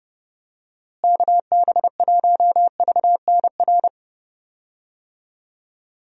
Morse code, heard as a keyed tone, about 30 wpm, 700 Hz. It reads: KB1VNR
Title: USA Call Signs - 6 Characters - Repeated 3x - 30wpm